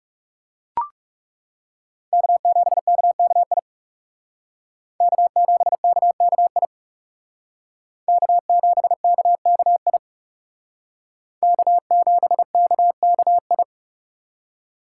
Morse code, heard as a keyed tone, about 38 wpm, 700 Hz. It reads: K7KKI K7KKI K7KKI K7KKI